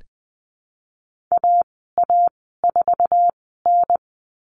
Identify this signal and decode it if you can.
Morse code, heard as a keyed tone, about 20 wpm, spaced as for 14 wpm, 700 Hz. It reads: AA4N
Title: 076 Lesson 13g - Letter L - Call Signs 20wpm